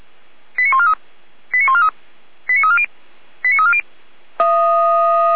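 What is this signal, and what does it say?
Five-tone selective call (ZVEI), heard as quick runs of five tones, tones of about 70 ms. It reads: 89143 89143 89340 89340
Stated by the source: ZVEI five-tone call received on SDR, from feuerwehr-orlamuende fire brigade audio server.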